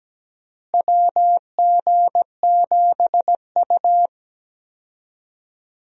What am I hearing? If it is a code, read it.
Morse code, heard as a keyed tone, about 17 wpm, 700 Hz. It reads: WG7U
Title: USA Call Signs - 4 Characters - Repeated 3x - 17wpm